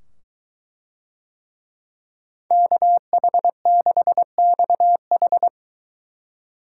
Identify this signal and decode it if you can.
Morse code, heard as a keyed tone, about 22 wpm, 700 Hz. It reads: KH6XH